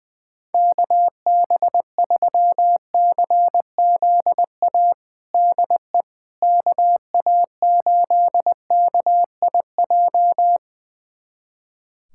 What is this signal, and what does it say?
Morse code, heard as a keyed tone, about 20 wpm, 700 Hz. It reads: KB3CZA DE KA8KIJ
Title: USA Callsign DE Callsign - 6 Characters - 20wpm